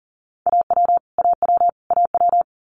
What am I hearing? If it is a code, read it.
Morse code, heard as a keyed tone, about 38 wpm, 700 Hz. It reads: AW AW AW
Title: Multi-letter Phonemes Repeated 3x 40wpm